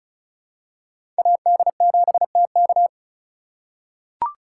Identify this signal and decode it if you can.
Morse code, heard as a keyed tone, about 34 wpm, 700 Hz. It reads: AD7TK
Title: USA Call Signs - 5 Characters - 3x Character Spacing - 35wpm